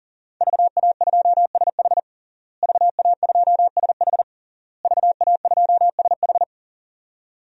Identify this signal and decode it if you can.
Morse code, heard as a keyed tone, about 40 wpm, 700 Hz. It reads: VA2SH VA2SH VA2SH
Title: DXCC-100 Call Signs - 5 Characters - Repeated 3x - 40wpm